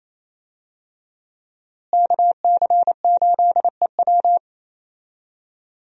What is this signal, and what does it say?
Morse code, heard as a keyed tone, about 28 wpm, 700 Hz. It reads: KC8EW
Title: USA Call Signs - 5 Characters - Repeated 3x - 28wpm